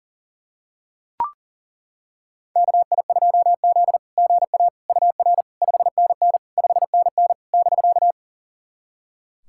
Morse code, heard as a keyed tone, about 40 wpm, 700 Hz. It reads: KI2Z GA UR 5NN 5NN <BK>
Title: SOTA QSO Practice 40wpm